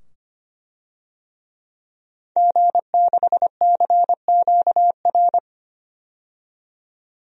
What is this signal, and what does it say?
Morse code, heard as a keyed tone, about 25 wpm, 700 Hz. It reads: G6CQR
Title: DXCC-100 Call Signs - 5 Characters - 25wpm